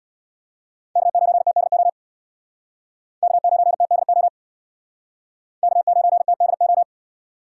Morse code, heard as a keyed tone, about 58 wpm, 700 Hz. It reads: K9TDO K9TDO K9TDO
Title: Speed Racing - USA 1x3 Call Signs 50wpm